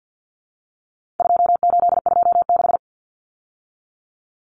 Morse code, heard as a keyed tone, about 48 wpm, 700 Hz. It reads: 2826
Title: Numbers - 4-Digit 50wpm